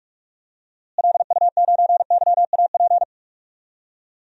Morse code, heard as a keyed tone, about 46 wpm, 700 Hz. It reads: LU9YAP
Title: DXCC-100 Call Signs - 6 Characters - Repeated 3x - 45wpm